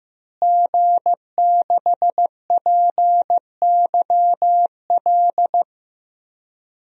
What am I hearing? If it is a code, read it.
Morse code, heard as a keyed tone, about 15 wpm, 700 Hz. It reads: G6PYL
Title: DXCC-100 Call Signs - 5 Characters - Repeated 2x - 15wpm